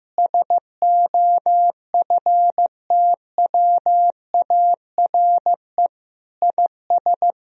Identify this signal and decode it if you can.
Morse code, heard as a keyed tone, about 15 wpm, 700 Hz. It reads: SOFTWARE IS
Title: Sentences from Top 1000 Words - Review Part 3 - 15wpm